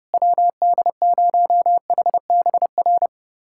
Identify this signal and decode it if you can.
Morse code, heard as a keyed tone, about 32 wpm, 690 Hz. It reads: WD0HBR
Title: USA Callsign DE Callsign - 6 Characters - 30wpm